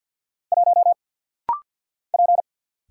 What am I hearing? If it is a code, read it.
Morse code, heard as a keyed tone, about 50 wpm, 700 Hz. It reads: 1 P
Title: Single Letter-Number Alt - Rapid-Fire 50wpm